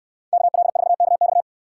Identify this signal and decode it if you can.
Morse code, heard as a keyed tone, about 65 wpm, 700 Hz. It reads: KL4KX